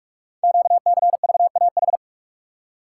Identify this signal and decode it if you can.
Morse code, heard as a keyed tone, about 44 wpm, 700 Hz. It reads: QCVAH
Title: Code Groups - 5 Letters - 45wpm